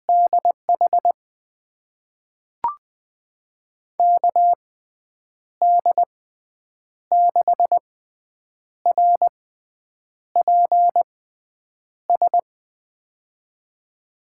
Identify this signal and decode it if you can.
Morse code, heard as a keyed tone, about 20 wpm, spaced as for 8 wpm, 700 Hz. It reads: DH KD6RPS
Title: USA Call Signs - 6 Characters - 6x Character Spacing - 20wpm